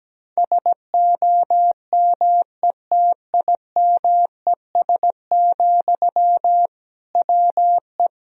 Morse code, heard as a keyed tone, about 17 wpm, 700 Hz. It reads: SOMETIMES, WE
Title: Sentences from Top 1000 Words - Review Part 3 - 17wpm